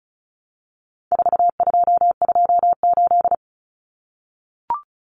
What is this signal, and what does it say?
Morse code, heard as a keyed tone, about 34 wpm, 700 Hz. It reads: 4228
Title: Numbers - 4-Digit 35wpm